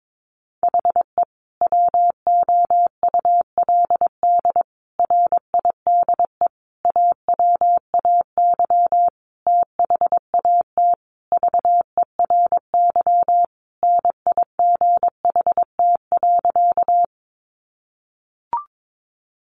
Morse code, heard as a keyed tone, about 22 wpm, 700 Hz. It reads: HE WOULD RIDE AWAY THAT VERY NIGHT.
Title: Sentences from Top 1200 Words 22wpm